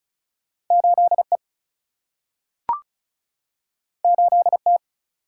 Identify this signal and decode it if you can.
Morse code, heard as a keyed tone, about 36 wpm, 700 Hz. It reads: 8E 8T
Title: Single-Letter Single-Number 35wpm